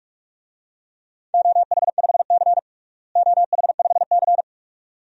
Morse code, heard as a keyed tone, about 46 wpm, 700 Hz. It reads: OH5C OH5C
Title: DXCC-100 Call Signs - 4 Characters - Repeated 2x - 45wpm